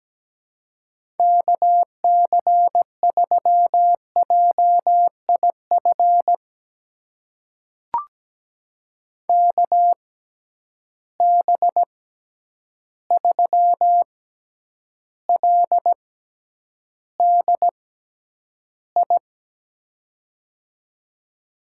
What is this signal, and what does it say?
Morse code, heard as a keyed tone, about 17 wpm, spaced as for 9 wpm, 700 Hz. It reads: KC3JIF KB3LDI
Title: USA Call Signs - 6 Characters - 6x Character Spacing - 17wpm